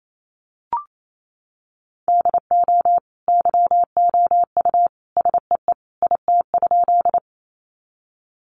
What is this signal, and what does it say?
Morse code, heard as a keyed tone, about 28 wpm, 700 Hz. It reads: DO YOU SEE IT?
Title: Sentences from Top 100 Words 28wpm